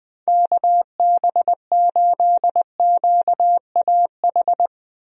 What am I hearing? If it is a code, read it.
Morse code, heard as a keyed tone, about 21 wpm, 700 Hz. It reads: KB8QAH
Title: USA Call Signs - 6 Characters - Repeated 2x - 20wpm